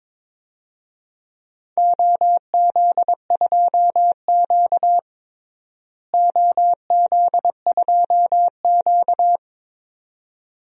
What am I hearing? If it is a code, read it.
Morse code, heard as a keyed tone, about 22 wpm, 700 Hz. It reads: OZ2Q OZ2Q
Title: DXCC-100 Call Signs - 4 Characters - Repeated 2x - 22wpm